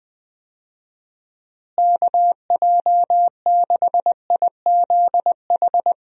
Morse code, heard as a keyed tone, about 20 wpm, 700 Hz. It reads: KJ6IZH
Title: USA Call Signs - 6 Characters - Repeated 2x - 20wpm